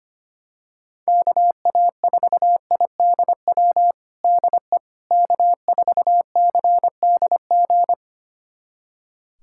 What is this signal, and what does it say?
Morse code, heard as a keyed tone, about 25 wpm, 700 Hz. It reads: KA4IDW DE K4CDG
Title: USA Callsign DE Callsign 25wpm